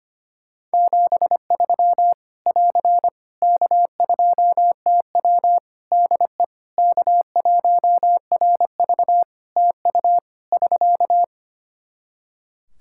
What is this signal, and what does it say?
Morse code, heard as a keyed tone, about 25 wpm, 700 Hz. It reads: 73 <AR> K2TW DE K1RV TU <SK>
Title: K1USN SST QSO Practice 25wpm